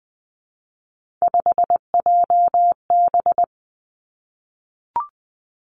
Morse code, heard as a keyed tone, about 20 wpm, 700 Hz. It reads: 5JB
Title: Three Challenging Letter-Number Combinations 20wpm